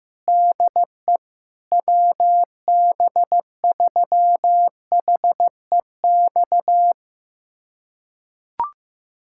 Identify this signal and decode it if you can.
Morse code, heard as a keyed tone, about 16 wpm, 700 Hz. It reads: DE WB3HEX
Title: USA Callsign DE Callsign - 6 Characters - 15wpm